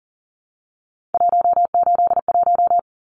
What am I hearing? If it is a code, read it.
Morse code, heard as a keyed tone, about 40 wpm, 700 Hz. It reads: 181
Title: Numbers - 3-Digit 40wpm